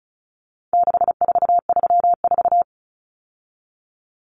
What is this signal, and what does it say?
Morse code, heard as a keyed tone, about 34 wpm, 700 Hz. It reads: 6434
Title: Numbers - 4-Digit 35wpm